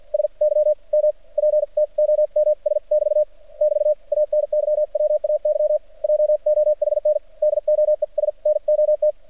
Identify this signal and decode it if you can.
Morse code, heard as a keyed tone, about 53 wpm, spaced as for 42 wpm, 600 Hz. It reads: SYMPTOMS <BT> <BT> ANYWAY JOHN DOES NOT
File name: archived / CRSnet